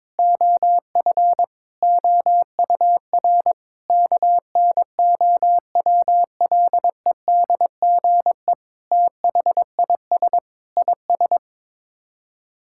Morse code, heard as a keyed tone, about 22 wpm, 700 Hz. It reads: OF OUR KNOWLEDGE THIS IS
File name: practice / 22wpm / Sets of 8 Words - 22wpm